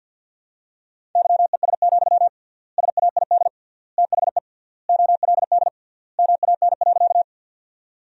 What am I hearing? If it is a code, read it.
Morse code, heard as a keyed tone, about 48 wpm, 700 Hz. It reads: YES, SAID THE OLD MAN.
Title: 3x Word-Spacing - Sentences from Top 500 Words - 50wpm